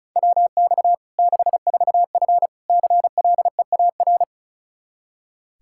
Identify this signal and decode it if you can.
Morse code, heard as a keyed tone, about 36 wpm, 700 Hz. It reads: WX 64F CLEAR
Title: Practice QSO - Exchange 2 - Info - 35wpm